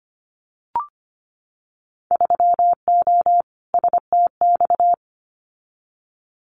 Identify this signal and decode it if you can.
Morse code, heard as a keyed tone, about 25 wpm, 690 Hz. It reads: 3O STX
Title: Winter Field Day Exchanges 25wpm